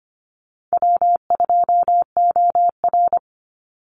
Morse code, heard as a keyed tone, about 25 wpm, 700 Hz. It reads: W2OR
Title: Easy Call Signs Repeated 2x - 25wpm